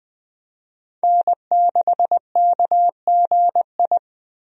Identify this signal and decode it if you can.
Morse code, heard as a keyed tone, about 21 wpm, 700 Hz. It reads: N6KGI
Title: USA Call Signs - 5 Characters - Repeated 2x - 20wpm